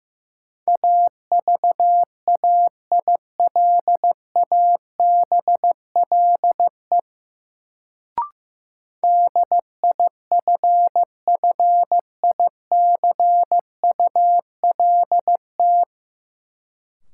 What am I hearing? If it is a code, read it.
Morse code, heard as a keyed tone, about 15 wpm, 700 Hz. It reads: AVAILABLE DIFFICULT
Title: Sentences from Top 600 Words - New Word Drill - 15wpm